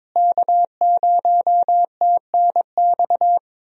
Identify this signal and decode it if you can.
Morse code, heard as a keyed tone, about 22 wpm, 700 Hz. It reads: K0TNX
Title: USA Call Signs - 5 Characters - Repeated 3x - 22wpm